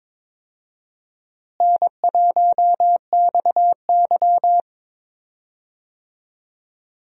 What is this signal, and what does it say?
Morse code, heard as a keyed tone, about 22 wpm, 700 Hz. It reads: N1XY
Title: USA Call Signs - 4 Characters - Repeated 3x - 22wpm